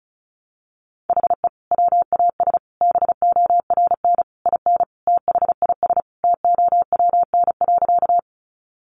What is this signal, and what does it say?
Morse code, heard as a keyed tone, about 34 wpm, 700 Hz. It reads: HE WAS BORN IN THIS TOWN.
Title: Sentences from Top 1000 Words 35wpm